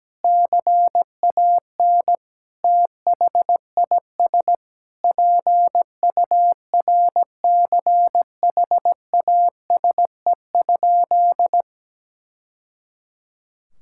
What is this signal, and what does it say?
Morse code, heard as a keyed tone, about 17 wpm, 700 Hz. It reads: CAN THIS PURCHASE?
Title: Sentences from Top 1000 Words - Review Part 5 - 17wpm